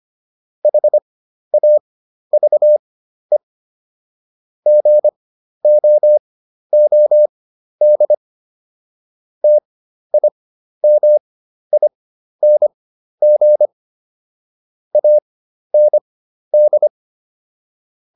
Morse code, heard as a keyed tone, about 25 wpm, spaced as for 12 wpm, 600 Hz. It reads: HAVE GOOD TIMING AND